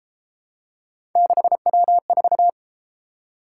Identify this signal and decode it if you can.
Morse code, heard as a keyed tone, about 32 wpm, 700 Hz. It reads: 6W4